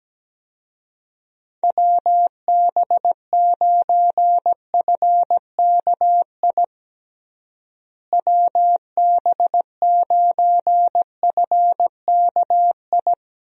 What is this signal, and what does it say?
Morse code, heard as a keyed tone, about 17 wpm, 700 Hz. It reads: WB9FKI WB9FKI
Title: USA Call Signs - 6 Characters - Repeated 3x - 17wpm